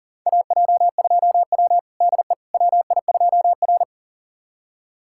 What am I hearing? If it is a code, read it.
Morse code, heard as a keyed tone, about 40 wpm, 700 Hz. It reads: AJ2W DE WI2R
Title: USA Callsign DE Callsign - 4 Characters - 40wpm